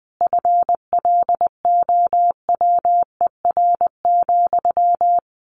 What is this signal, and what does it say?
Morse code, heard as a keyed tone, about 20 wpm, 700 Hz. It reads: FLOWER,
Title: A Sound Of Thunder 20wpm